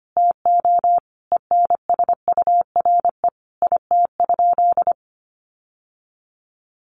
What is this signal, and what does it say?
Morse code, heard as a keyed tone, about 25 wpm, 690 Hz. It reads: TO ENSURE IT?
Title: Sentences from Top 1100 Words 25wpm